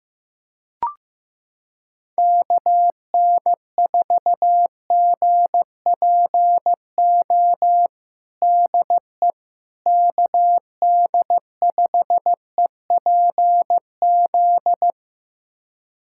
Morse code, heard as a keyed tone, about 16 wpm, 700 Hz. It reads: KN4GPO DE KD5EPZ
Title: USA Callsign DE Callsign - 6 Characters - 15wpm